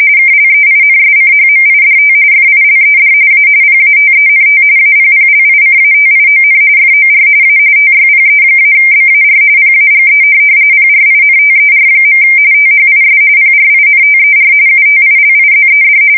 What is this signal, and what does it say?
RTTY, about 45.45 baud. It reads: KLEINSCHMIDT FILED A PATENT APPLICATION FOR A TYPE-BAR PAGE PRINTER. THIS PRINTER UTILIZED
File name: fsk170 / teletype-corp